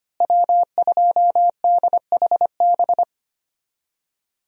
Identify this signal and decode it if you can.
Morse code, heard as a keyed tone, about 25 wpm, 700 Hz. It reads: W2DHB
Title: USA Call Signs - 5 Characters - Repeated 3x - 25wpm